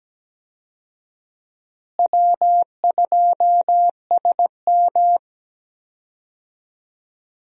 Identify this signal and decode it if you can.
Morse code, heard as a keyed tone, about 17 wpm, 700 Hz. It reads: W2SM
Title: USA Call Signs - 4 Characters - Repeated 3x - 17wpm